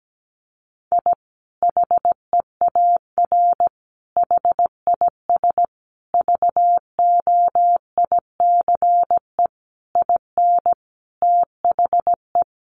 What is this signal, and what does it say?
Morse code, heard as a keyed tone, about 17 wpm, 700 Hz. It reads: I HEAR HIS VOICE IN THE
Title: Sentences from Top 500 Words 17wpm